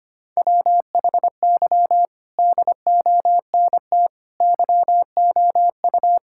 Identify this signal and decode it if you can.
Morse code, heard as a keyed tone, about 25 wpm, 700 Hz. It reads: WHY DONT YOU
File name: practice / 25wpm / Sentences with 4-Word Phrases - 25wpm